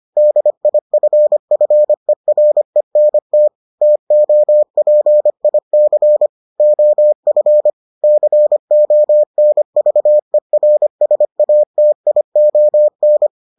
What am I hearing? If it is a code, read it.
Morse code, heard as a keyed tone, about 25 wpm, 600 Hz. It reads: DIFFERENT TOPIC OF CONVERSATION